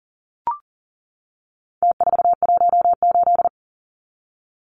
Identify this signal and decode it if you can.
Morse code, heard as a keyed tone, about 40 wpm, 700 Hz. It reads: T418
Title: Numbers - Contest Style - 4-Digit 40wpm